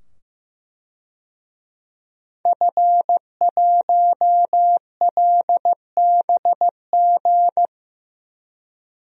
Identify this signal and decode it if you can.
Morse code, heard as a keyed tone, about 16 wpm, 700 Hz. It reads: F1LBG
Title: DXCC-100 Call Signs - 5 Characters - 15wpm